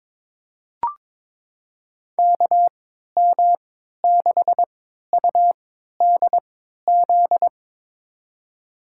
Morse code, heard as a keyed tone, about 21 wpm, spaced as for 12 wpm, 700 Hz. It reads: KM6UDZ
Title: USA Call Signs - 3x Character Spacing - 22wpm